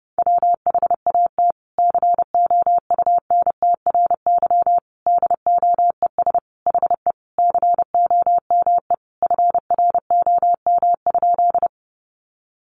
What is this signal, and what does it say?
Morse code, heard as a keyed tone, about 30 wpm, 700 Hz. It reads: WHAT COUNTRY DOES HE COME FROM?
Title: Sentences from Top 300 Words 30wpm